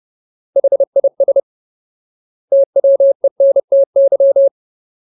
Morse code, heard as a keyed tone, about 30 wpm, 550 Hz. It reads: HIS TWENTY
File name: Ultimate Word List / 30WPM Ultimate